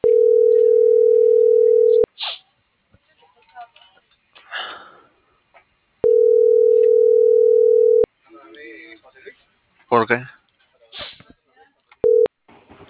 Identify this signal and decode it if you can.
Ringback tone (North America), 440 + 480 Hz, with speech.
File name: CENTRAL MABE